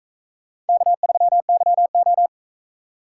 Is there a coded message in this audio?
Yes: Morse code from 0.7 to 2.3 s, heard as a keyed tone, about 40 wpm, 700 Hz. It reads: K3YO